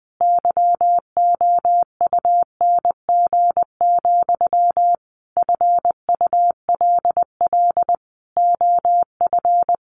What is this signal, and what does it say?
Morse code, heard as a keyed tone, about 20 wpm, 700 Hz. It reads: YOUNG, FULL OF
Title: Sentences from Top 1100 Words 20wpm